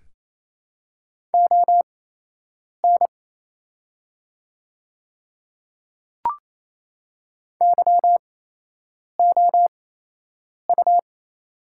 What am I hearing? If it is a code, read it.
Morse code, heard as a keyed tone, about 27 wpm, spaced as for 8 wpm, 700 Hz. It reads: ON YOU
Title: 8x Character Spacing - Top 200 Words - 28wpm